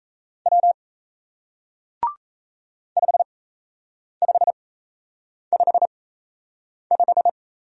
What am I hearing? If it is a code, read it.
Morse code, heard as a keyed tone, about 37 wpm, spaced as for 8 wpm, 700 Hz. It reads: W 5555